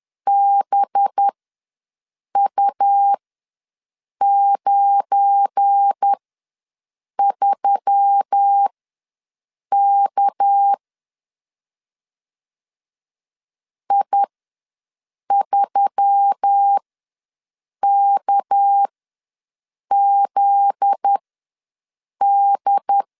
Morse code, heard as a keyed tone, about 10 wpm, spaced as for 6 wpm, 800 Hz. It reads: BU93K I3KZD
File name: LEZIONI / 7 GF9Z / 7 GF9Z ws 10wpm 3919